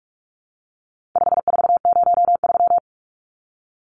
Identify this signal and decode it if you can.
Morse code, heard as a keyed tone, about 44 wpm, 700 Hz. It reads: 5403
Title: Numbers - 4-Digit 45wpm